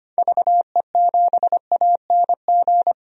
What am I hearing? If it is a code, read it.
Morse code, heard as a keyed tone, about 25 wpm, 700 Hz. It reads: VE7ANG
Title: DXCC-100 Call Signs - 6 Characters - Repeated 3x - 25wpm